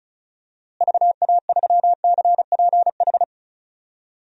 Morse code, heard as a keyed tone, about 36 wpm, 700 Hz. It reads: VA3CPH
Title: DXCC-100 Call Signs - 6 Characters - Repeated 3x - 35wpm